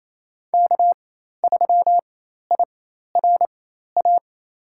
Morse code, heard as a keyed tone, about 28 wpm, spaced as for 13 wpm, 700 Hz. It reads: K3IRA
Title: USA Call Signs - 5 Characters - 4x Character Spacing - 28wpm